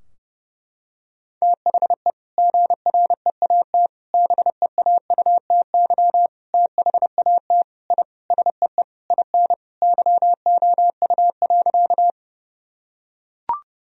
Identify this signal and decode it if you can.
Morse code, heard as a keyed tone, about 32 wpm, 700 Hz. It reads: THE GREAT BEAUTY THAT I SEE IN YOU.
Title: Sentences from Top 1000 Words - Review Part 2 - 30wpm